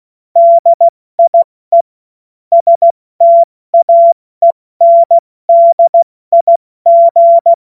Morse code, heard as a keyed tone, about 16 wpm, spaced as for 14 wpm, 680 Hz. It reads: DIE STAENDIG
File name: Witz-16-September-2025-70-F80